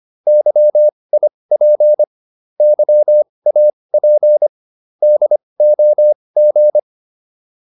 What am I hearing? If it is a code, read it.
Morse code, heard as a keyed tone, about 25 wpm, spaced as for 20 wpm, 600 Hz. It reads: YIP YAP DOG